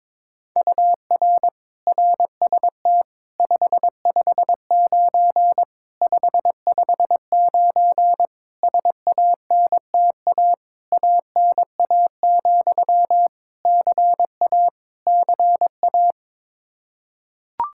Morse code, heard as a keyed tone, about 22 wpm, 700 Hz. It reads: UR RST 559 559 SANTA ANA, CA CA
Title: Practice QSO - Exchange 1 - Beginning - 22wpm